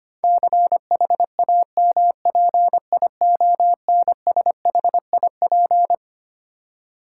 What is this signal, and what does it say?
Morse code, heard as a keyed tone, about 25 wpm, 700 Hz. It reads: CHAMPIONSHIP
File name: Twelve Letter Words - 25wpm